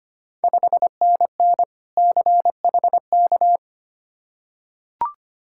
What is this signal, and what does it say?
Morse code, heard as a keyed tone, about 25 wpm, 700 Hz. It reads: 5NN CHK
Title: Ontario QSO Party Exchange 25wpm